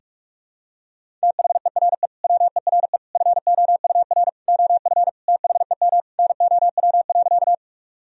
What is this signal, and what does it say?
Morse code, heard as a keyed tone, about 46 wpm, 700 Hz. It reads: THERE WERE FOUR OF THEM NOW.
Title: Sentences from Top 1000 Words - Review Part 1 - 45wpm